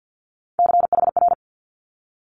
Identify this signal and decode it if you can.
Morse code, heard as a keyed tone, about 50 wpm, 700 Hz. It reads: CHR
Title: Multi-letter Phonemes 50wpm